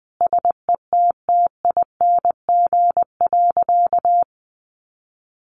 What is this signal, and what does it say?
Morse code, heard as a keyed tone, about 20 wpm, 700 Hz. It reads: SETTING.
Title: Sentences from Top 1300 Words 20wpm